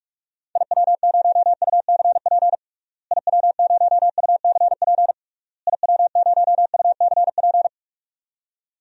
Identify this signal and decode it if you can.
Morse code, heard as a keyed tone, about 41 wpm, 700 Hz. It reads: IW0UCP IW0UCP IW0UCP
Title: DXCC-100 Call Signs - 6 Characters - Repeated 3x - 45wpm